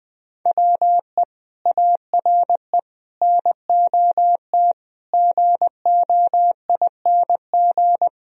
Morse code, heard as a keyed tone, about 20 wpm, 700 Hz. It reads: WE ARE NOT GOING